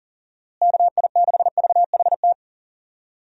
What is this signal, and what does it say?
Morse code, heard as a keyed tone, about 40 wpm, 700 Hz. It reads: KI6VHT
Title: USA Call Signs - 6 Characters - Repeated 3x - 40wpm